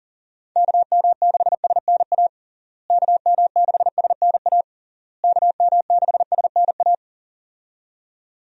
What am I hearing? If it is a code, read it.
Morse code, heard as a keyed tone, about 40 wpm, 700 Hz. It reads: KM6SNA KM6SNA KM6SNA
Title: USA Call Signs - 6 Characters - Repeated 3x - 40wpm